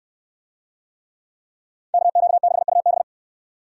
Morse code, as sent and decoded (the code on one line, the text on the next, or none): -.- --.- -.... ..- -...
KQ6UB